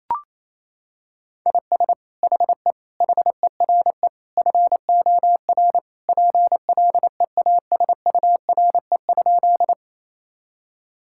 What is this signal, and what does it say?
Morse code, heard as a keyed tone, about 28 wpm, 700 Hz. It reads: IS HE HERE FOR PLEASURE?
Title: Sentences from Top 1600 Words 28wpm